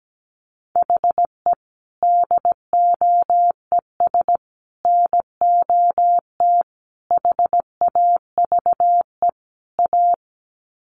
Morse code, heard as a keyed tone, about 17 wpm, 700 Hz. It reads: HE DOES NOT HAVE A
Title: Sets of 5 Words Encore 17wpm